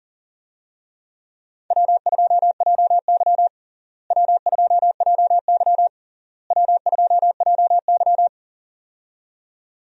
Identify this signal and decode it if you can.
Morse code, heard as a keyed tone, about 40 wpm, 700 Hz. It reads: W2JY W2JY W2JY
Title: USA Call Signs - 4 Characters - Repeated 3x - 40wpm